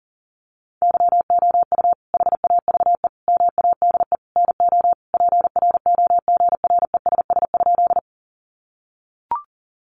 Morse code, heard as a keyed tone, about 40 wpm, 700 Hz. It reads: YOU HAVE MADE NO PROGRESS?
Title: Sentences from Top 1400 Words 40wpm